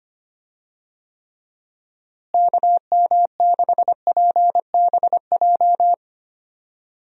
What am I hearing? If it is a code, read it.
Morse code, heard as a keyed tone, about 25 wpm, 700 Hz. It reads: KM6PBJ
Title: USA Call Signs - 6 Characters - Repeated 2x - 25wpm